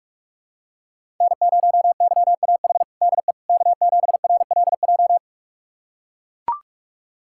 Morse code, heard as a keyed tone, about 43 wpm, 700 Hz. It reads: N0YAH DE K7RLJ
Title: USA Callsign DE Callsign - 5 Characters - 45wpm